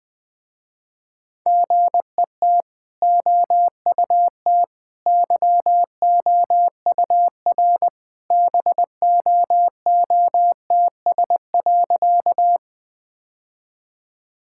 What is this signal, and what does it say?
Morse code, heard as a keyed tone, about 21 wpm, 700 Hz. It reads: GET OUT YOUR BOOTS.
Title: Easy Sentences with Hints - 20wpm